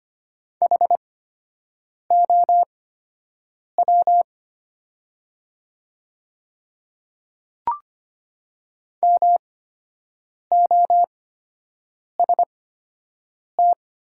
Morse code, heard as a keyed tone, about 25 wpm, spaced as for 7 wpm, 700 Hz. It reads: HOW MOST
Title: 8x Character Spacing - Top 200 Words - 25wpm